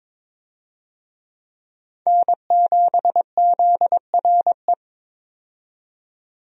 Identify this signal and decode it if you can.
Morse code, heard as a keyed tone, about 21 wpm, 700 Hz. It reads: N7ZRE
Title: USA Call Signs - 5 Characters - 22wpm